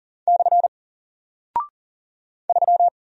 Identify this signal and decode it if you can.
Morse code, heard as a keyed tone, about 40 wpm, 700 Hz. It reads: / 3
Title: Single Character (Letter:Number:Punctuation:Prosign) Alt 40wpm